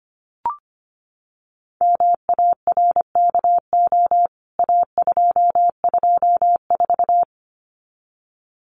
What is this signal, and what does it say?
Morse code, heard as a keyed tone, about 25 wpm, 700 Hz. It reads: MARKO A224
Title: CWT Exchange 25wpm